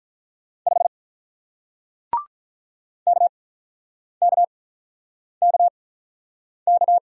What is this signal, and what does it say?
Morse code, heard as a keyed tone, about 48 wpm, spaced as for 9 wpm, 700 Hz. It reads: 5 KKKK